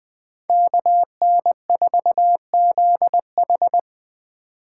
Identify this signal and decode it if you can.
Morse code, heard as a keyed tone, about 20 wpm, 700 Hz. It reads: KN4ZH
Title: Speed Racing - USA 2x2 Call Signs 17wpm